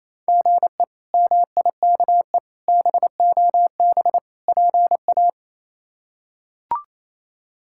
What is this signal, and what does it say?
Morse code, heard as a keyed tone, about 28 wpm, 700 Hz. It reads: GE MIKE BOB PA
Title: K1USN SST QSO Practice - 28wpm